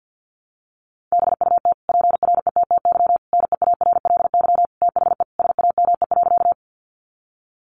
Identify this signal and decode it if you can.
Morse code, heard as a keyed tone, about 50 wpm, 700 Hz. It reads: BUT PRETTY NEARLY THE SAME.
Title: Sentences from Top 1300 Words 50wpm